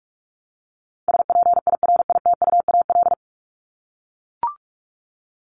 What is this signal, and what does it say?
Morse code, heard as a keyed tone, about 44 wpm, 700 Hz. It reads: SPIRITUAL
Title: Top 1501-1750 Words 45wpm